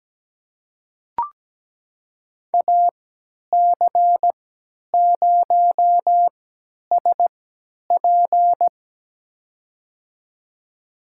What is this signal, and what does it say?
Morse code, heard as a keyed tone, about 17 wpm, spaced as for 10 wpm, 700 Hz. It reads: AC0SP
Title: USA Call Signs - 5 Characters - 3x Character Spacing - 17wpm